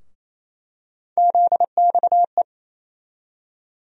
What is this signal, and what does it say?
Morse code, heard as a keyed tone, about 28 wpm, 700 Hz. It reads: ZXE
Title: USA Call Sign Postfixes - 3 Letters 28wpm